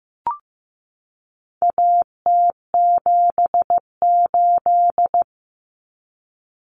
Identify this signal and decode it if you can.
Morse code, heard as a keyed tone, about 15 wpm, 700 Hz. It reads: AT78
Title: Common Cut Numbers - 4-Digit 15wpm